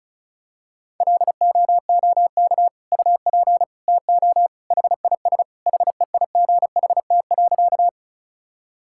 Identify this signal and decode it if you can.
Morse code, heard as a keyed tone, about 36 wpm, 700 Hz. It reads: LOOK UP TO HIS HEIGHT.
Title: Sentences from Top 1600 Words 35wpm